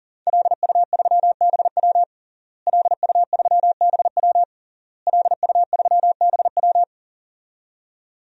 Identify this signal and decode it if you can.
Morse code, heard as a keyed tone, about 37 wpm, 700 Hz. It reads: LU3BW LU3BW LU3BW
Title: DXCC-100 Call Signs - 5 Characters - Repeated 3x - 40wpm